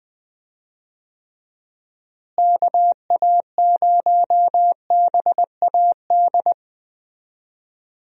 Morse code, heard as a keyed tone, about 20 wpm, 700 Hz. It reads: KA0BAD
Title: USA Call Signs - 6 Characters - Repeated 2x - 20wpm